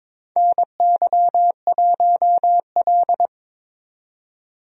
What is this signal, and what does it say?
Morse code, heard as a keyed tone, about 22 wpm, 700 Hz. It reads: NY1L
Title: USA Call Signs - 4 Characters - 22wpm